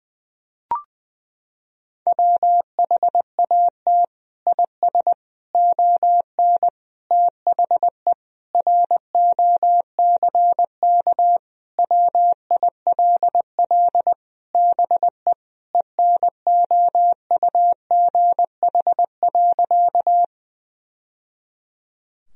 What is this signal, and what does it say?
Morse code, heard as a keyed tone, about 21 wpm, 700 Hz. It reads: WHAT IS ON THE ROCK WILL BE ENOUGH.
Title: Sentences from Top 1000 Words - Review Part 2 - 20wpm